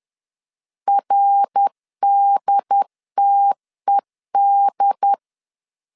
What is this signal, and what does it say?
Morse code, heard as a keyed tone, about 10 wpm, 800 Hz. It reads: RDTED
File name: CW / DET / 1 3 1500C S B 10wpm 3335